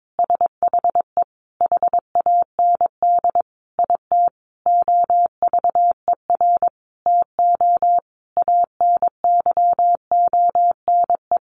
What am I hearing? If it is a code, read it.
Morse code, heard as a keyed tone, about 22 wpm, 700 Hz. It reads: SHE HAND IT OVER TO ANYONE
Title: Sentences from Top 900 Words 22wpm